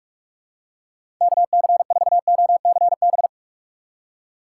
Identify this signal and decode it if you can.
Morse code, heard as a keyed tone, about 46 wpm, 700 Hz. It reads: KC4OCB
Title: USA Call Signs - 6 Characters - 3x Character Spacing - 45wpm